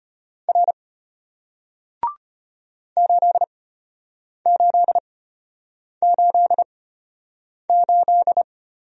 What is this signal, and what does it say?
Morse code, heard as a keyed tone, about 34 wpm, spaced as for 8 wpm, 690 Hz. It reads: R 8888